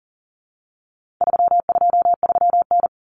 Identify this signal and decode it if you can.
Morse code, heard as a keyed tone, about 40 wpm, 700 Hz. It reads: 323N
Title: Common Cut Numbers - 4-Digit 40wpm